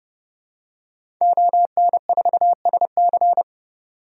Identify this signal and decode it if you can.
Morse code, heard as a keyed tone, about 30 wpm, 700 Hz. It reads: ON4SC
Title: DXCC-100 Call Signs - 5 Characters - 30wpm